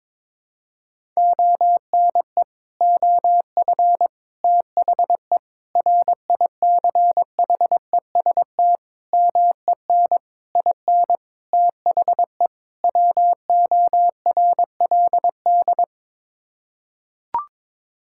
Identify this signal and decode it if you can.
Morse code, heard as a keyed tone, about 22 wpm, 700 Hz. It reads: ONE OF THE RICHEST MEN IN THE WORLD
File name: practice / 22wpm / Sets of 8 Words - 22wpm